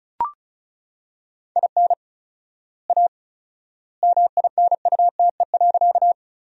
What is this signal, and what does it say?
Morse code, heard as a keyed tone, about 34 wpm, 700 Hz. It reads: IN A MINUTE.
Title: 4x Word-Spacing - Sentences from Top 500 Words - 35wpm